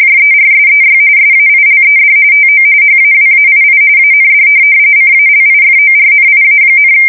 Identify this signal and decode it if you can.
RTTY, about 45.45 baud. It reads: JERSEY SAILED TO HAWAII AND CALIFORNIA,
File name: fsk170 / USS-New-Jersey